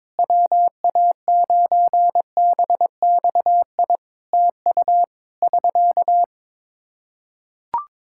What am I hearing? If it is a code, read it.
Morse code, heard as a keyed tone, about 22 wpm, 700 Hz. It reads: WA9BXI TU <SK>
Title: Practice QSO - Exchange 4 - Ending - 22wpm